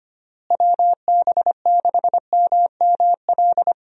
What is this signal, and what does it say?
Morse code, heard as a keyed tone, about 25 wpm, 700 Hz. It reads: WB6MML
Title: USA Call Signs - 6 Characters - Repeated 3x - 25wpm